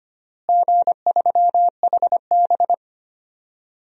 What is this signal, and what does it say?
Morse code, heard as a keyed tone, about 25 wpm, 700 Hz. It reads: G3HB
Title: DXCC-100 Call Signs - 4 Characters - Repeated 3x - 25wpm